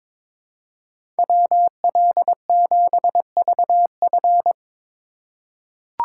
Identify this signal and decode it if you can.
Morse code, heard as a keyed tone, about 22 wpm, 700 Hz. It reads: WL7VF
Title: USA Call Signs - 5 Characters - 3x Character Spacing - 22wpm